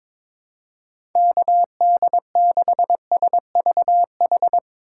Morse code, heard as a keyed tone, about 22 wpm, 700 Hz. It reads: KD6SVH
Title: USA Call Signs - 6 Characters - 4x Character Spacing - 22wpm